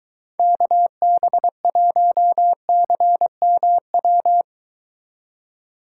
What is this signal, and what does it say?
Morse code, heard as a keyed tone, about 22 wpm, 700 Hz. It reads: KB1CMW